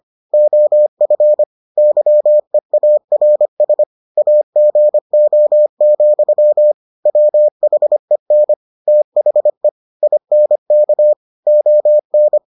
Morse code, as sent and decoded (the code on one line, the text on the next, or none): --- ..-. / -.-- . .- .-. ... / .- --. --- --..-- / .-- .... -. / - .... / .. -. -.- / --- -.
OF YEARS AGO, WHEN THE INK ON